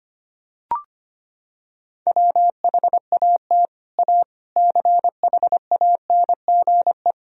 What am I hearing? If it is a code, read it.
Morse code, heard as a keyed tone, about 25 wpm, 700 Hz. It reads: WHAT A CHANGE
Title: Sentences from Top 1000 Words - Review Part 1 - 25wpm